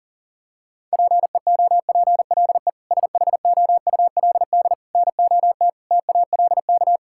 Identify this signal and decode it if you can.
Morse code, heard as a keyed tone, about 40 wpm, 700 Hz. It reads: PEOPLE SHOULD NOT TALK
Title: Sentences from Top 1000 Words - Review Part 6 - 40wpm